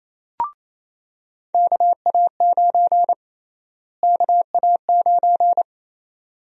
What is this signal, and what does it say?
Morse code, heard as a keyed tone, about 28 wpm, 700 Hz. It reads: KA9 KA9
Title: USA Call Sign Prefixes - Repeated 2x - 28wpm